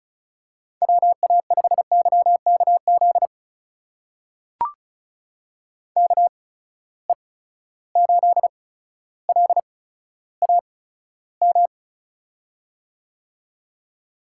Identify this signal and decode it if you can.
Morse code, heard as a keyed tone, about 36 wpm, spaced as for 15 wpm, 700 Hz. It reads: WA5YKZ KE8LAM
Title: USA Call Signs - 6 Characters - 8x Character Spacing - 35wpm